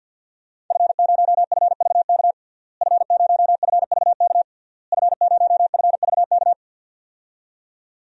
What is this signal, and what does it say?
Morse code, heard as a keyed tone, about 52 wpm, 700 Hz. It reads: F0FVK F0FVK F0FVK
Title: DXCC-100 Call Signs - 5 Characters - Repeated 3x - 50wpm